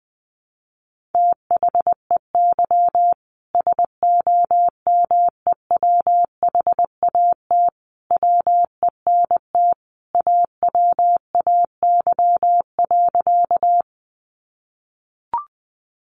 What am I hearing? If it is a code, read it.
Morse code, heard as a keyed tone, about 20 wpm, 700 Hz. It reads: THEY SOMEWHAT WENT AWAY.
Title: Sentences from Top 1300 Words 20wpm